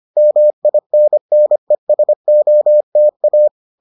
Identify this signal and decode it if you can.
Morse code, heard as a keyed tone, about 25 wpm, 600 Hz. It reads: MINNESOTA